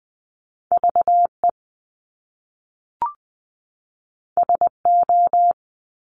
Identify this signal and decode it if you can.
Morse code, heard as a keyed tone, about 20 wpm, 700 Hz. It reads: V E S O
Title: Top Two Letter Combinations in Words 20wpm